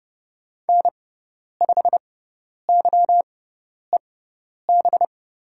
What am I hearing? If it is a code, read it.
Morse code, heard as a keyed tone, about 30 wpm, spaced as for 10 wpm, 700 Hz. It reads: N5YEB
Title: USA Call Signs - 5 Characters - 6x Character Spacing - 30wpm